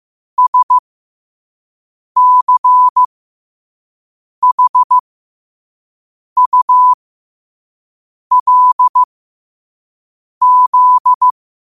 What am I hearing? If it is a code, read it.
Morse code, heard as a keyed tone, about 15 wpm, spaced as for 5 wpm, 1000 Hz. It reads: SCHULZ
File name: qotdfiles / 5wpm